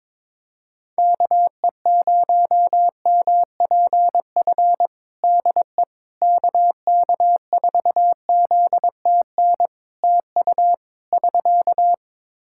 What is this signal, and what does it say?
Morse code, heard as a keyed tone, about 22 wpm, 700 Hz. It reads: KE0MPF DE KK4ZTN TU <SK>
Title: Practice QSO - Exchange 4 - Ending - 22wpm